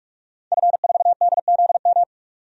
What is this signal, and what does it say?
Morse code, heard as a keyed tone, about 46 wpm, 700 Hz. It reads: F4DZM
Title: DXCC-100 Call Signs - 5 Characters - 45wpm